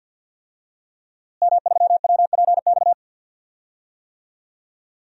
Morse code, heard as a keyed tone, about 46 wpm, 700 Hz. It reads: M3WPX
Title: DXCC-100 Call Signs - 5 Characters - Repeated 3x - 50wpm